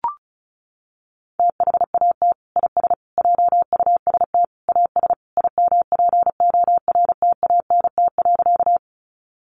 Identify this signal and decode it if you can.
Morse code, heard as a keyed tone, about 34 wpm, 700 Hz. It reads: THAT IS JUST AS IMPORTANT.
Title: Sentences from Top 600 Words 35wpm